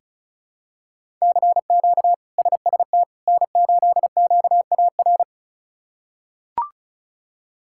Morse code, heard as a keyed tone, about 36 wpm, 700 Hz. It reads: CQ SST N8QAR
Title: K1USN SST QSO Practice - 35wpm